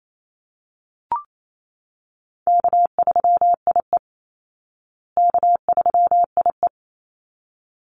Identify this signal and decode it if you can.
Morse code, heard as a keyed tone, about 28 wpm, 700 Hz. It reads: K3IE K3IE
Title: Easy Call Signs Repeated 2x - 28wpm